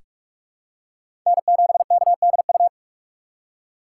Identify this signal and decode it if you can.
Morse code, heard as a keyed tone, about 46 wpm, 700 Hz. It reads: N7KDU
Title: USA Call Signs - 5 Characters - 3x Character Spacing - 45wpm